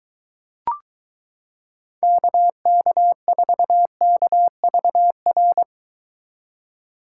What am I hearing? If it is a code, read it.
Morse code, heard as a keyed tone, about 23 wpm, 700 Hz. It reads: KK4KVR